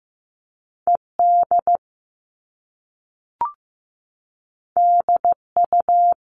Morse code, heard as a keyed tone, about 15 wpm, 700 Hz. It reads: ED DU
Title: All Cut Numbers - 2-Digit 15wpm